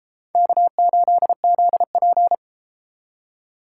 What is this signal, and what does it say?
Morse code, heard as a keyed tone, about 31 wpm, 700 Hz. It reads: K8ZP